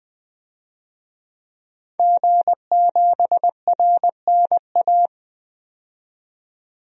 Morse code, heard as a keyed tone, about 20 wpm, 700 Hz. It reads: G7RNA
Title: DXCC-100 Call Signs - 5 Characters - 20wpm